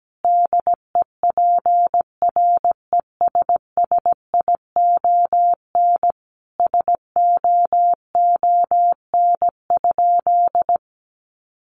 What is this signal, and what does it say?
Morse code, heard as a keyed tone, about 17 wpm, 700 Hz. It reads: DEPRESSION SOON?
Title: Sentences from Top 1300 Words 17wpm